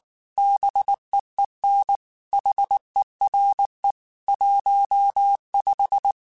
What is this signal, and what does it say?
Morse code, heard as a keyed tone, about 19 wpm, 800 Hz. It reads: BEEN HERE 15
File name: QSOs / QSOs-mp3s / WD0BGP 1@18